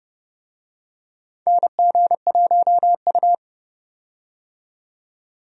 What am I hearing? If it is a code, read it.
Morse code, heard as a keyed tone, about 30 wpm, 700 Hz. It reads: NG1U
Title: USA Call Signs - 4 Characters - 30wpm